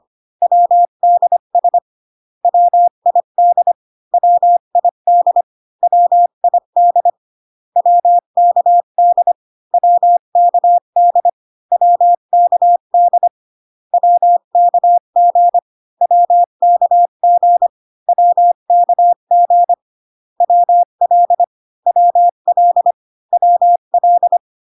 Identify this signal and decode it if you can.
Morse code, heard as a keyed tone, about 25 wpm, 700 Hz. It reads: WDS WID WID WID WKD WKD WKD WKG WKG WKG WL WL WL